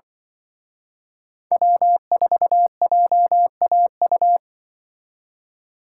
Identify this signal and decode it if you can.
Morse code, heard as a keyed tone, about 25 wpm, 700 Hz. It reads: W4JAU
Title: Speed Racing - USA Call Signs 20wpm